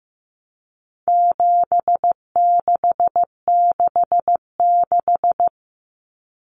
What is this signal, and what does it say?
Morse code, heard as a keyed tone, about 15 wpm, 700 Hz. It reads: 7666
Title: Numbers - 4-Digit 15wpm